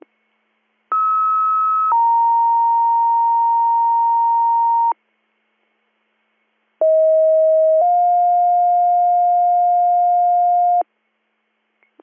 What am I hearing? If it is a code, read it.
Two-tone page, heard as a one-second tone, then a three-second tone, about 1290 then 930 Hz, about 640 then 710 Hz. It is heard from 0.9 to 10.8 s.